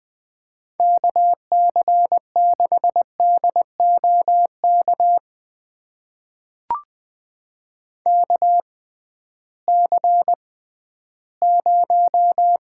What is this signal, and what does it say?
Morse code, heard as a keyed tone, about 20 wpm, spaced as for 13 wpm, 700 Hz. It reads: KC6DOK KC0
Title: USA Call Signs - 6 Characters - 6x Character Spacing - 20wpm